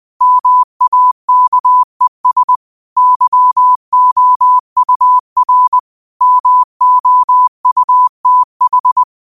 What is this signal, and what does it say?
Morse code, heard as a keyed tone, about 20 wpm, 1000 Hz. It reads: MAKES YOUR MOUTH